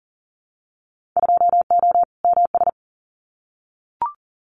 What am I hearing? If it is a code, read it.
Morse code, heard as a keyed tone, about 40 wpm, 700 Hz. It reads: 2O MS
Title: Winter Field Day Exchanges 40wpm